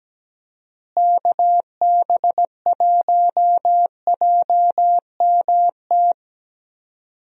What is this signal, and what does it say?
Morse code, heard as a keyed tone, about 17 wpm, 700 Hz. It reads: KB1JMT
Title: USA Call Signs - 6 Characters - Repeated 3x - 17wpm